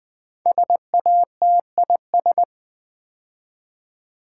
Morse code, heard as a keyed tone, about 20 wpm, 700 Hz. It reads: SATIS
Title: Build A Word - 5 Letters At A Time - 12-Letter Words - 20wpm